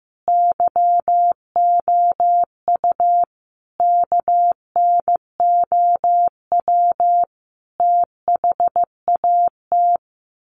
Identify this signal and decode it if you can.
Morse code, heard as a keyed tone, about 15 wpm, 700 Hz. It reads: YOU KNOW THAT
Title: Sets of 5 Words Encore 15wpm